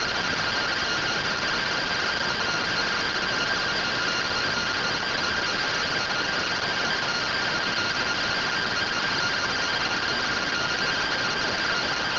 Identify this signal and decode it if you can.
Two-tone teleprinter signal, about 170 Hz shift.
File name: findings / MR AXIAL fr1 120.gen-26 rtty rx